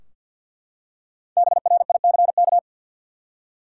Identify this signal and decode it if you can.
Morse code, heard as a keyed tone, about 53 wpm, 700 Hz. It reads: BRICK
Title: Five Letter Words - 50wpm